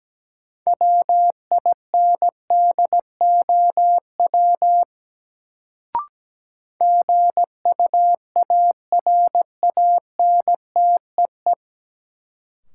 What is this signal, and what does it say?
Morse code, heard as a keyed tone, about 17 wpm, 700 Hz. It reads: WINDOW GUARANTEE
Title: Sentences from Top 1600 Words - New Word Drill Alt - 17wpm